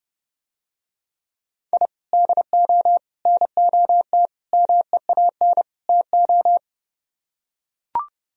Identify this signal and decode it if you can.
Morse code, heard as a keyed tone, about 32 wpm, 700 Hz. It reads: I DO NOT MEAN TO